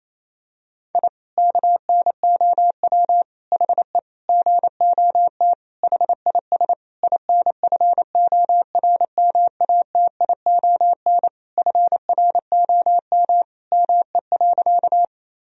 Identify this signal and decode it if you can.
Morse code, heard as a keyed tone, about 27 wpm, 700 Hz. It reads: I KNOW HE GOT HIS INFORMATION FROM ME.
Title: Sentences from Top 1000 Words - Review Part 3 - 28wpm